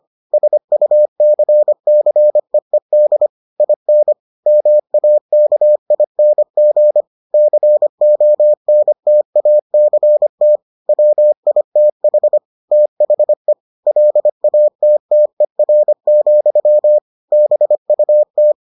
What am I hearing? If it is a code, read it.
Morse code, heard as a keyed tone, about 25 wpm, 600 Hz. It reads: SUCCEED IN MAKING CONTACT WITH THE LATTER, BUT